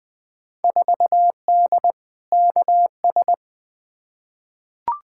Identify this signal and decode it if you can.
Morse code, heard as a keyed tone, about 21 wpm, 700 Hz. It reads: 4D KS
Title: Field Day Exchanges Encore - 20wpm